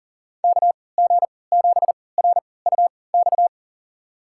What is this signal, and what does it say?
Morse code, heard as a keyed tone, about 40 wpm, spaced as for 23 wpm, 700 Hz. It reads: KG7RUX
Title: USA Call Signs - 6 Characters - 3x Character Spacing - 40wpm